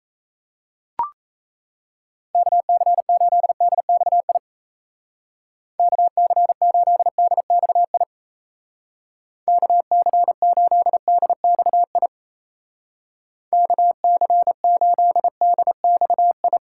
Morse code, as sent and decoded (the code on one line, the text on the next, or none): -.- -.-. ---.. -.. -..- .. / -.- -.-. ---.. -.. -..- .. / -.- -.-. ---.. -.. -..- .. / -.- -.-. ---.. -.. -..- ..
KC8DXI KC8DXI KC8DXI KC8DXI